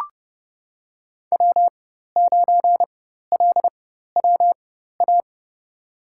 Morse code, heard as a keyed tone, about 30 wpm, spaced as for 14 wpm, 700 Hz. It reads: W9LWA
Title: USA Call Signs - 5 Characters - 4x Character Spacing - 30wpm